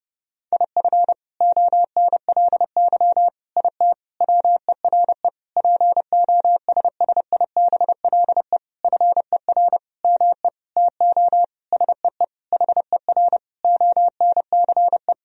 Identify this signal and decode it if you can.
Morse code, heard as a keyed tone, about 30 wpm, 700 Hz. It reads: IF ONLY IT WERE POSSIBLE FER ME TO SEE HER ONCE
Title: Sentences from Top 500 Words and Abbreviations 30wpm